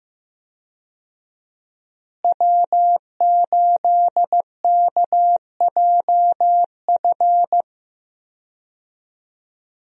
Morse code, heard as a keyed tone, about 15 wpm, 700 Hz. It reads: W8KJF
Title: USA Call Signs - 5 Characters - Repeated 3x - 15wpm